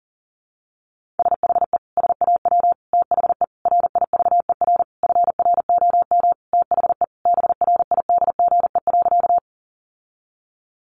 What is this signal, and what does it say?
Morse code, heard as a keyed tone, about 40 wpm, 700 Hz. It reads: SHE SAW THE RIVER FROM THE BRIDGE.
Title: Sentences from Top 1200 Words 40wpm